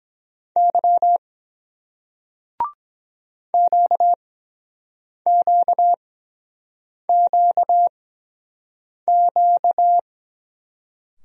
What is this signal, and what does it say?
Morse code, heard as a keyed tone, about 23 wpm, spaced as for 7 wpm, 700 Hz. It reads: Y QQQQ